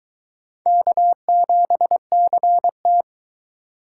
Morse code, heard as a keyed tone, about 22 wpm, 700 Hz. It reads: K7CT